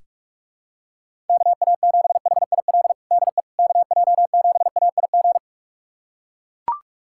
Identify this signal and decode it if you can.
Morse code, heard as a keyed tone, about 43 wpm, 700 Hz. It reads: KA7HIL DE KJ7AIG
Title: USA Callsign DE Callsign - 6 Characters - 45wpm